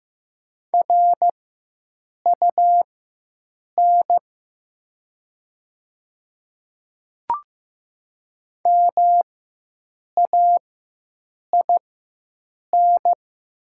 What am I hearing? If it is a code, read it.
Morse code, heard as a keyed tone, about 15 wpm, spaced as for 7 wpm, 700 Hz. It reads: RUN MAIN
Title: 4x Character Spacing - Top 300 Words - 15wpm